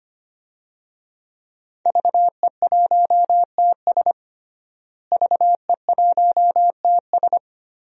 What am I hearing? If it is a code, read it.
Morse code, heard as a keyed tone, about 25 wpm, 700 Hz. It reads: VE1TS VE1TS
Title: DXCC-100 Call Signs - 5 Characters - Repeated 3x - 25wpm